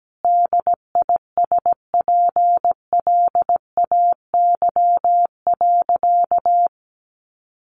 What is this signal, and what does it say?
Morse code, heard as a keyed tone, about 17 wpm, 700 Hz. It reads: DISPLAY.
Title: Sentences from Top 1200 Words 17wpm